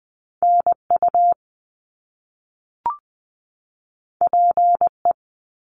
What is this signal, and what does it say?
Morse code, heard as a keyed tone, about 20 wpm, 700 Hz. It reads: NU PE
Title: Canadian Provinces and Territories 20wpm